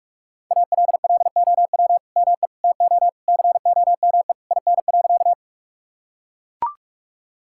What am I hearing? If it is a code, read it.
Morse code, heard as a keyed tone, about 44 wpm, 700 Hz. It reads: ALLOW ME TO COME IN.
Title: Sentences from Top 1000 Words - Review Part 5 - 45wpm